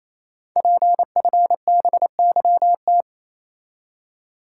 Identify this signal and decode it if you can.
Morse code, heard as a keyed tone, about 27 wpm, 700 Hz. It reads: PFBYT
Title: Code Groups - 5 Letters - 28wpm